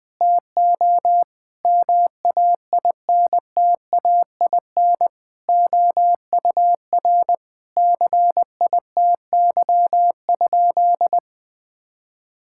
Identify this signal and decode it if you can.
Morse code, heard as a keyed tone, about 20 wpm, 700 Hz. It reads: TO MAINTAIN OUR CITY?
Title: Sentences from Top 1000 Words - Review Part 5 - 20wpm